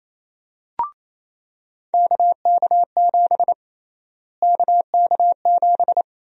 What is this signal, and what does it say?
Morse code, heard as a keyed tone, about 28 wpm, 700 Hz. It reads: KK7 KK7
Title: USA Call Sign Prefixes - Repeated 2x - 28wpm